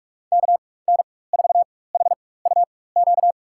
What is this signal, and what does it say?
Morse code, heard as a keyed tone, about 47 wpm, spaced as for 21 wpm, 700 Hz. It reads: KN4HUQ
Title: USA Call Signs - 4x Character Spacing - 45wpm